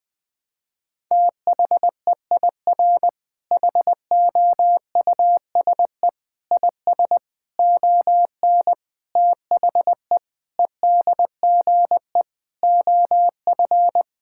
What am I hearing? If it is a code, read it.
Morse code, heard as a keyed tone, about 20 wpm, 700 Hz. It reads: THEIR HOUSE IS ON THE EDGE OF
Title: Sentences with 5-Word Phrases - Spoken Beginning - 20wpm